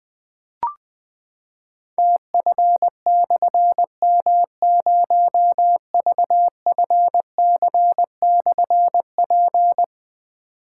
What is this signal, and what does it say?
Morse code, heard as a keyed tone, about 20 wpm, 700 Hz. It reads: TF/M0VFC/P
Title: Difficult Call Signs 20wpm